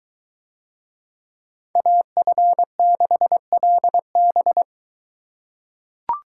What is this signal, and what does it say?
Morse code, heard as a keyed tone, about 21 wpm, 700 Hz. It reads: AF6LB